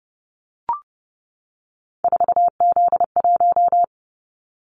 Morse code, heard as a keyed tone, about 30 wpm, 700 Hz. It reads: 4Z1
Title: Callsign Prefixes for Top 50 DXCC - 30wpm